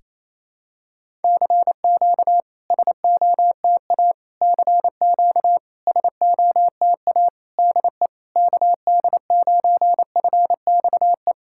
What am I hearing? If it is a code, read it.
Morse code, heard as a keyed tone, about 28 wpm, 690 Hz. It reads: CQ SOTA CQ SOTA DE KD9FXE
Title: SOTA QSO Practice 28wpm